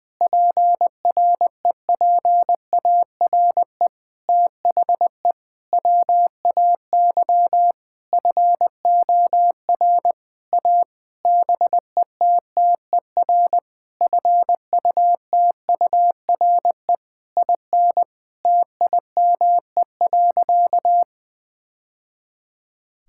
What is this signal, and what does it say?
Morse code, heard as a keyed tone, about 20 wpm, 700 Hz. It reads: PREPARE THE WAY FOR A BETTER FUTURE IN TIME.
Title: Sentences from Top 1000 Words - Review Part 6 - 20wpm